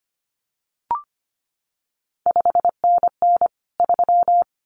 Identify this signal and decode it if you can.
Morse code, heard as a keyed tone, about 25 wpm, 700 Hz. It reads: 5NN 3
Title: CQ WW DX Contest Exchange Encore - 25wpm